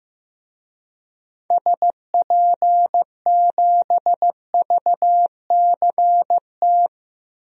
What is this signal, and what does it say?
Morse code, heard as a keyed tone, about 15 wpm, 700 Hz. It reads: SP7VCT
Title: DXCC-100 Call Signs - 6 Characters - 15wpm